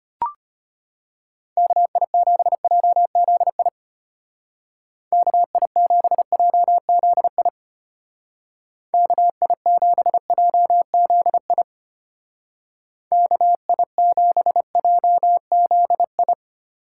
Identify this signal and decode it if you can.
Morse code, heard as a keyed tone, about 31 wpm, 700 Hz. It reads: KI7JZI KI7JZI KI7JZI KI7JZI